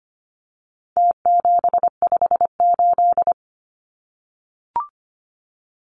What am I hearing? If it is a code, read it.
Morse code, heard as a keyed tone, about 25 wpm, 700 Hz. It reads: T758
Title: Numbers - Contest Style - 4-Digit 25wpm